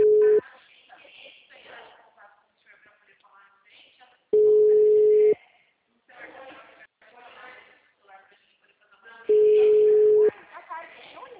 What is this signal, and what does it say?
Ringback tone (Europe, 425 Hz), with speech.
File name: out- -2011- - - .77105